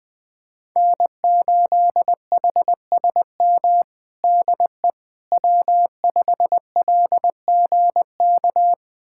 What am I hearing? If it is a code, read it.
Morse code, heard as a keyed tone, about 20 wpm, 700 Hz. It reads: N8HSM DE W5LGK
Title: USA Callsign DE Callsign - 5 Characters - 20wpm